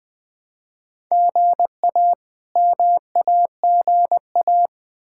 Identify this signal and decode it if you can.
Morse code, heard as a keyed tone, about 21 wpm, 700 Hz. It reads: GA MAGA
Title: K1USN SST QSO Practice - 20wpm